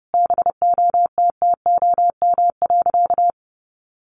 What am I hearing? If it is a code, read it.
Morse code, heard as a keyed tone, about 30 wpm, 700 Hz. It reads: BOTTOM.
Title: Sentences from Top 1400 Words 30wpm